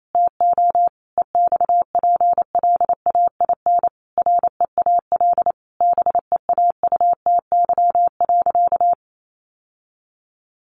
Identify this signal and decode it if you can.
Morse code, heard as a keyed tone, about 28 wpm, 700 Hz. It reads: TO EXPLAIN REAL BEAUTY.
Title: Sentences from Top 1100 Words 28wpm